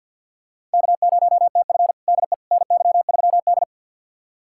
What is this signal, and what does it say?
Morse code, heard as a keyed tone, about 53 wpm, 700 Hz. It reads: K0TF DE NY3D
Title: USA Callsign DE Callsign - 4 Characters - 50wpm